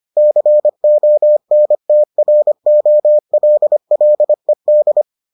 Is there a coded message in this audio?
Yes: Morse code from 0.2 to 5.0 s, heard as a keyed tone, about 25 wpm, 600 Hz. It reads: CONTROLLED